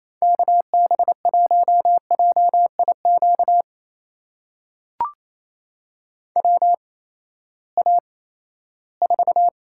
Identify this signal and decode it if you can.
Morse code, heard as a keyed tone, about 28 wpm, spaced as for 16 wpm, 700 Hz. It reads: KB1JIQ WA4
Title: USA Call Signs - 6 Characters - 8x Character Spacing - 28wpm